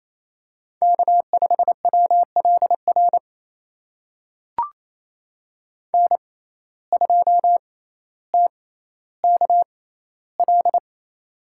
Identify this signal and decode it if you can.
Morse code, heard as a keyed tone, about 28 wpm, spaced as for 14 wpm, 700 Hz. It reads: K5WLR N2TKL
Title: USA Call Signs - 5 Characters - 6x Character Spacing - 28wpm